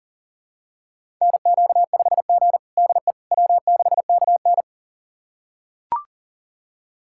Morse code, heard as a keyed tone, about 37 wpm, 700 Hz. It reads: NQ5G DE W6KN
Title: USA Callsign DE Callsign - 4 Characters - 40wpm